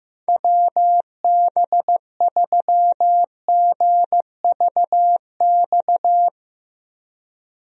Morse code, heard as a keyed tone, about 15 wpm, 700 Hz. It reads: WB3GVX
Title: USA Call Signs - 6 Characters - Repeated 2x - 15wpm